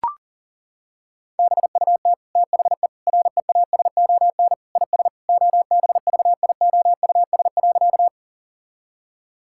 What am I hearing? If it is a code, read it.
Morse code, heard as a keyed tone, about 40 wpm, 700 Hz. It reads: BUT THE REASON IS OBVIOUS.
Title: Sentences from Top 1600 Words 40wpm